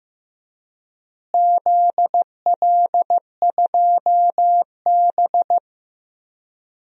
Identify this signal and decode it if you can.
Morse code, heard as a keyed tone, about 15 wpm, 700 Hz. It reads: ZL2B
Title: DXCC-100 Call Signs - 4 Characters - Repeated 2x - 15wpm